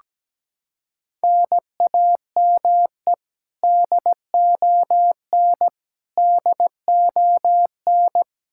Morse code, heard as a keyed tone, about 17 wpm, 700 Hz. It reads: NAME DON DON
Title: Practice QSO - Exchange 1 - Beginning - 17wpm